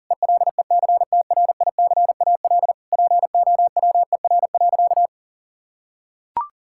Morse code, heard as a keyed tone, about 40 wpm, 700 Hz. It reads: ELECTRICAL POWER.
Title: Sentences from Top 1000 Words - Review Part 6 - 40wpm